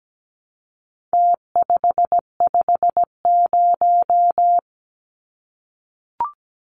Morse code, heard as a keyed tone, about 17 wpm, 700 Hz. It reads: T550
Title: Numbers - Contest Style - 4-Digit 17wpm